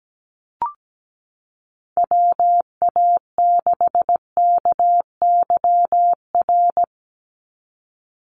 Morse code, heard as a keyed tone, about 17 wpm, 700 Hz. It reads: WA6KYR
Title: NA SOTA Chaser Callsigns 17wpm